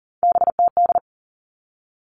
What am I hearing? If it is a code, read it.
Morse code, heard as a keyed tone, about 40 wpm, 700 Hz. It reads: BTD
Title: All Cut Numbers - 3-Digit 40wpm